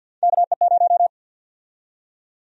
Morse code, heard as a keyed tone, about 51 wpm, 700 Hz. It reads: KE0
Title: USA Call Sign Prefixes - 50wpm